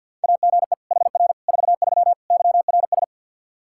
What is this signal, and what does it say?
Morse code, heard as a keyed tone, about 52 wpm, 700 Hz. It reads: AGE HR 43 YRS
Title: K1USN SST QSO Practice 50wpm